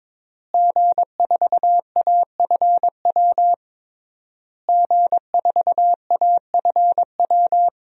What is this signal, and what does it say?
Morse code, heard as a keyed tone, about 21 wpm, 700 Hz. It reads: G4AFW G4AFW
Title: DXCC-100 Call Signs - 5 Characters - Repeated 3x - 22wpm